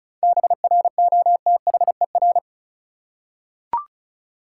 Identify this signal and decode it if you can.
Morse code, heard as a keyed tone, about 34 wpm, 700 Hz. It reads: BROTHER
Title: Seven Letter Words - 35wpm